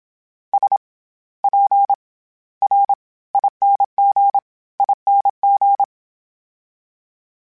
Morse code, heard as a keyed tone, about 27 wpm, 800 Hz. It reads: S P R ING ING
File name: Special INGs-mp3s / Special INGs-mp3s / SlowToFaster ing @25